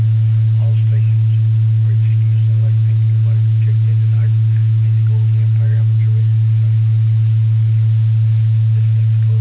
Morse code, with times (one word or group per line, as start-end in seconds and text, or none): none